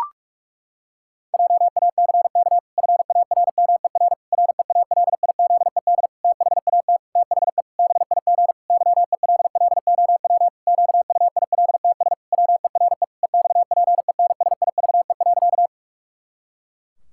Morse code, heard as a keyed tone, about 41 wpm, 700 Hz. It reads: JACK FARMER REALIZED THAT THE BIG YELLOW QUILTS WERE EXPENSIVE.
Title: Pangram Challenge v2 - 45wpm